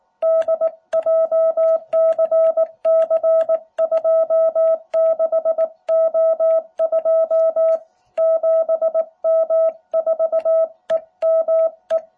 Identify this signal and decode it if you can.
Morse code, heard as a keyed tone, about 19 wpm, 650 Hz. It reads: D JCC26O2 7M4EME